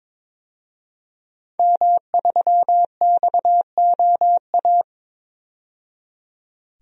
Morse code, heard as a keyed tone, about 22 wpm, 700 Hz. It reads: M3XOA
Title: DXCC-100 Call Signs - 5 Characters - 22wpm